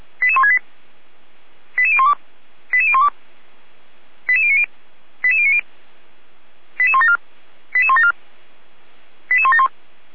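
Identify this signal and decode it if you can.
Five-tone selective call (ZVEI), heard as quick runs of five tones, tones of about 70 ms. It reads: 88168 88012 88012 88090 88090 88174 88174 88171
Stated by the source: Five-tone ZVEI selective call received by SDR, from fire brigade Feuerwehr Orlamünde's audio archive.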